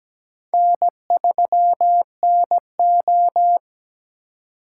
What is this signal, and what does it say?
Morse code, heard as a keyed tone, about 17 wpm, 700 Hz. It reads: N3NO
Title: USA Call Signs - 4 Characters - 3x Character Spacing - 17wpm